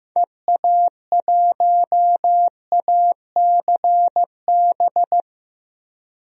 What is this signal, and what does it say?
Morse code, heard as a keyed tone, about 15 wpm, 700 Hz. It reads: EA1ACB
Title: DXCC-100 Call Signs - 6 Characters - Repeated 3x - 15wpm